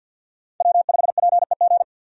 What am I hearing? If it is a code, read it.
Morse code, heard as a keyed tone, about 52 wpm, 700 Hz. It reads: W5PEG
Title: USA Call Signs - 5 Characters - 50wpm